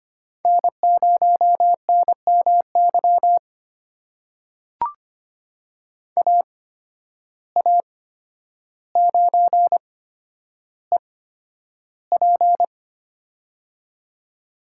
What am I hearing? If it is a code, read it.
Morse code, heard as a keyed tone, about 25 wpm, spaced as for 11 wpm, 700 Hz. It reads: N0NMY AA9EP
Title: USA Call Signs - 5 Characters - 8x Character Spacing - 25wpm